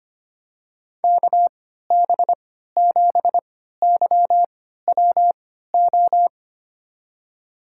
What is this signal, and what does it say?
Morse code, heard as a keyed tone, about 25 wpm, spaced as for 14 wpm, 700 Hz. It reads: KB7YWO
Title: USA Call Signs - 6 Characters - 3x Character Spacing - 25wpm